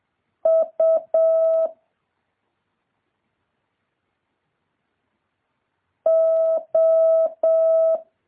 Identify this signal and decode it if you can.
Morse code, heard as a keyed tone, about 7 wpm, 650 Hz. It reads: U O
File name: jh8cbh / 2023cw